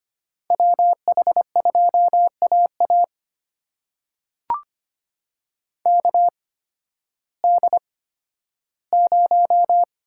Morse code, heard as a keyed tone, about 25 wpm, spaced as for 13 wpm, 700 Hz. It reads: WH2AA KD0
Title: USA Call Signs - 8x Character Spacing - 25wpm